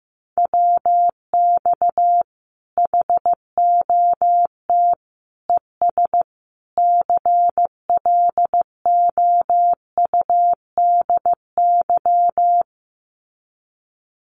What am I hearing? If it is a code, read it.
Morse code, heard as a keyed tone, about 15 wpm, 700 Hz. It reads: WX HOT ES CLOUDY
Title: Common QSO Phrases 15wpm